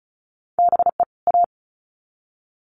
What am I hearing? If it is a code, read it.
Morse code, heard as a keyed tone, about 34 wpm, 700 Hz. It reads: BE A
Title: Sets of 2 Words 35wpm